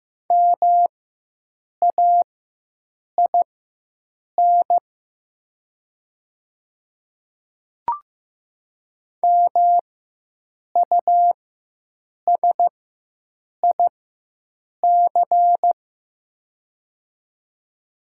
Morse code, heard as a keyed tone, about 15 wpm, spaced as for 7 wpm, 700 Hz. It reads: MAIN MUSIC
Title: 4x Character Spacing - Top 300 Words - 15wpm